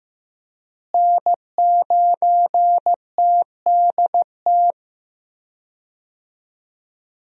Morse code, heard as a keyed tone, about 15 wpm, 700 Hz. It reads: N9TDT
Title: USA Call Signs - 5 Characters - Repeated 3x - 15wpm